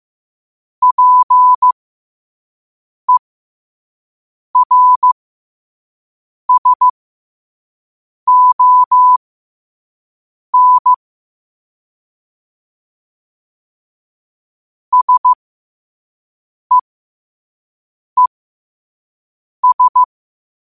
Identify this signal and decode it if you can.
Morse code, heard as a keyed tone, about 15 wpm, spaced as for 5 wpm, 1000 Hz. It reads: PERSON SEES
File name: qotdfiles / 5wpm